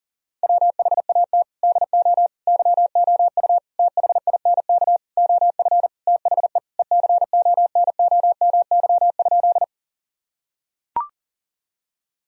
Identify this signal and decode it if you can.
Morse code, heard as a keyed tone, about 40 wpm, 700 Hz. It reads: WHAT DO YOU THINK OF THE ECONOMY?
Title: Sentences from Top 1000 Words - Review Part 3 - 40wpm